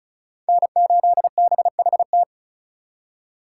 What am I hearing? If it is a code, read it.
Morse code, heard as a keyed tone, about 43 wpm, 700 Hz. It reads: N8BHT
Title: Speed Racing - USA Call Signs 35wpm